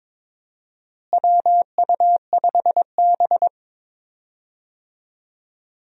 Morse code, heard as a keyed tone, about 23 wpm, 700 Hz. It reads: WU5B
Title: USA Call Signs - 4 Characters - Repeated 2x - 22wpm